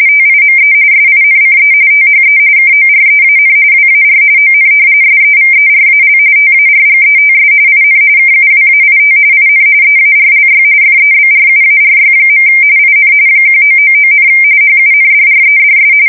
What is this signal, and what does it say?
RTTY, about 45.45 baud. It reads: TIME SINCE THE KOREAN WAR ON SEVERAL TARGETS IN KUWAIT IN EARLY FEBRUARY. IN 1999, HUNDREDS
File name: fsk170 / usnavy5